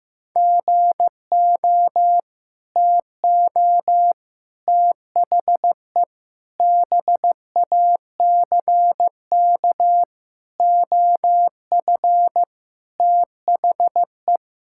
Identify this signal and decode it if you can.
Morse code, heard as a keyed tone, about 15 wpm, 700 Hz. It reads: GO TO THE BACK OF THE
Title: Sentences from Top 1000 Words - Review Part 5 - 15wpm